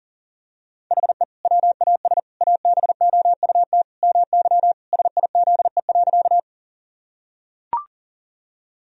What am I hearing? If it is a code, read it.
Morse code, heard as a keyed tone, about 40 wpm, 700 Hz. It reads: HE WAS ABOUT MY SIZE.
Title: Sentences from Top 1000 Words - Review Part 2 - 40wpm